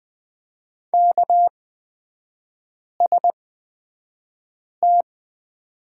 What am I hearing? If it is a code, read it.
Morse code, heard as a keyed tone, about 20 wpm, spaced as for 5 wpm, 700 Hz. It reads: KST
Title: 243 Lesson 39e - Letter z - Three Characters 20wpm